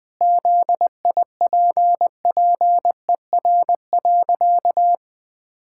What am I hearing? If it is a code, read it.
Morse code, heard as a keyed tone, about 20 wpm, 700 Hz. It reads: ZIPPER.
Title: Pangram Challenge v2 - 20wpm